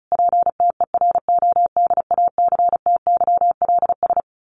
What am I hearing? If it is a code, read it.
Morse code, heard as a keyed tone, about 34 wpm, 700 Hz. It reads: PTERODACTYLS
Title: A Sound Of Thunder 35wpm